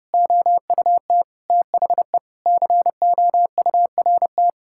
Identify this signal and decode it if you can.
Morse code, heard as a keyed tone, about 30 wpm, 700 Hz. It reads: OUT THE COURT
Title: Sentences from Top 1600 Words 30wpm